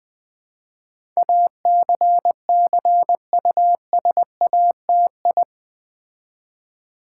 Morse code, heard as a keyed tone, about 20 wpm, 700 Hz. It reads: ACCUSATI
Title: Build A Word - 4 Letters At A Time - 10-Letter Words - 20wpm